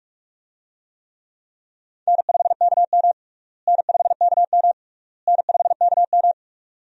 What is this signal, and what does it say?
Morse code, heard as a keyed tone, about 44 wpm, 700 Hz. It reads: N5KM N5KM N5KM
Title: USA Call Signs - 4 Characters - Repeated 3x - 45wpm